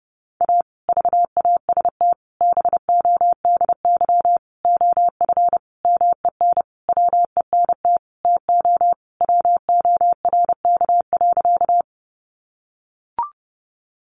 Morse code, heard as a keyed tone, about 30 wpm, 700 Hz. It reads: A VAST BODY OF MEN WENT TO WORK.
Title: Sentences from Top 1300 Words 30wpm